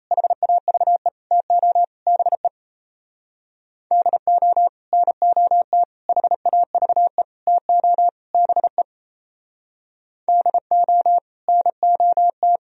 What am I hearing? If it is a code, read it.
Morse code, heard as a keyed tone, about 36 wpm, 700 Hz. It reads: HAVE TO BE DO NOT HAVE TO BE DO NOT